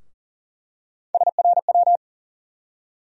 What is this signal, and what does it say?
Morse code, heard as a keyed tone, about 40 wpm, 700 Hz. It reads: SRW
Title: USA Call Sign Postfixes - 3 Letters 40wpm